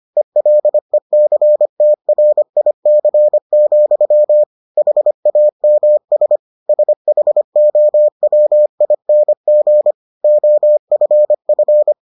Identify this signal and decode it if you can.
Morse code, heard as a keyed tone, about 25 wpm, 600 Hz. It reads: ELECTRIC, HAMS SHOWING OFF